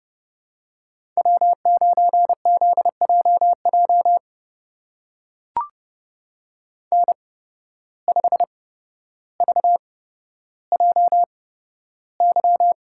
Morse code, heard as a keyed tone, about 32 wpm, spaced as for 13 wpm, 700 Hz. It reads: W9ZJJ N5VJY
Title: USA Call Signs - 5 Characters - 8x Character Spacing - 30wpm